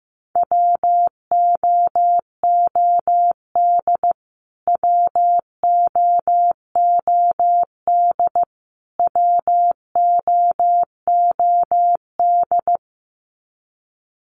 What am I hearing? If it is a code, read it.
Morse code, heard as a keyed tone, about 15 wpm, 700 Hz. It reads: WOOD WOOD WOOD
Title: Top 500 Words Repeated 15wpm